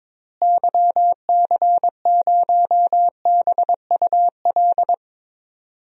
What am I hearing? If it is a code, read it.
Morse code, heard as a keyed tone, about 22 wpm, 700 Hz. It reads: YC0BUL
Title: DXCC-100 Call Signs - 6 Characters - Repeated 3x - 22wpm